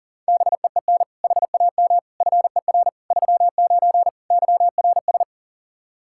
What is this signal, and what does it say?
Morse code, heard as a keyed tone, about 40 wpm, 700 Hz. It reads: BEEN HAM FER 39 YRS
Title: K1USN SST QSO Practice 40wpm